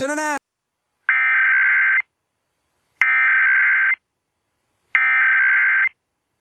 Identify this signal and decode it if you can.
Emergency Alert System (SAME) tones (header bursts), heard as harsh buzzing bursts, with speech. It is heard from 1.1 to 5.8 s.